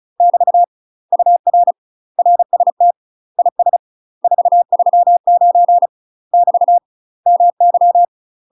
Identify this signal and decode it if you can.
Morse code, heard as a keyed tone, about 36 wpm, 700 Hz. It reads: <BT> UR RST IS 439 <BT> MY
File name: UK MASTER QSOs TRAINING / 35WPM / UK MASTER QSOs 0029